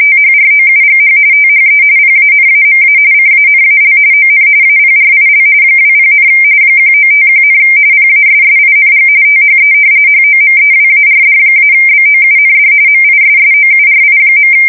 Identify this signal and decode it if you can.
RTTY, about 45.45 baud. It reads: THROUGH THE PANAMA CANAL ON APRIL 20, 1951, PASSING THE HOMEWARD-BOUND MISSOURI.
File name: fsk170 / battleshipnj